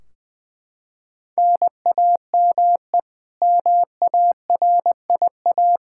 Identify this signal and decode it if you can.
Morse code, heard as a keyed tone, about 20 wpm, 700 Hz. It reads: NAME MARIA
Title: Practice QSO - Exchange 1 - Beginning - 20wpm